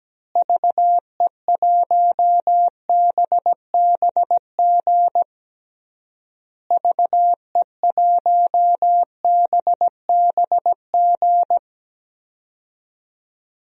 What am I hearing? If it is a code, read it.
Morse code, heard as a keyed tone, about 17 wpm, 700 Hz. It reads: VE1BBG VE1BBG
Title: DXCC-100 Call Signs - 6 Characters - Repeated 3x - 17wpm